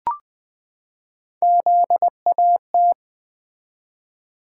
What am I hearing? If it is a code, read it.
Morse code, heard as a keyed tone, about 20 wpm, 700 Hz. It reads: ZAT
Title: Three Letter-Number Combinations - 20wpm